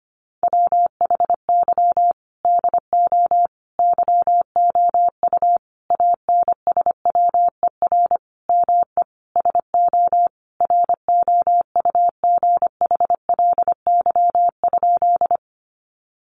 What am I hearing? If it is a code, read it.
Morse code, heard as a keyed tone, about 25 wpm, 700 Hz. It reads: WHY DO YOU ANSWER ME SO ROUGHLY?
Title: Sentences from Top 1500 Words 25wpm